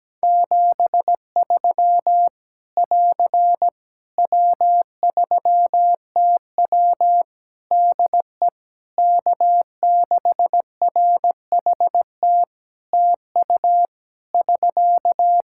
Morse code, heard as a keyed tone, about 17 wpm, 700 Hz. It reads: 73 <AR> W3TW DE K6RHT TU <SK>
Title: Practice QSO - Exchange 4 - Ending - 17wpm